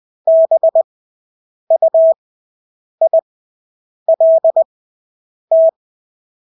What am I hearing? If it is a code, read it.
Morse code, heard as a keyed tone, about 20 wpm, spaced as for 8 wpm, 650 Hz. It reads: BUILT
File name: Engelske ord 20/8 wpm